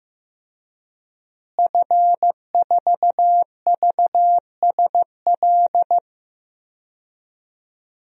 Morse code, heard as a keyed tone, about 15 wpm, 700 Hz. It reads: F4VSL
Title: DXCC-100 Call Signs - 5 Characters - Repeated 2x - 15wpm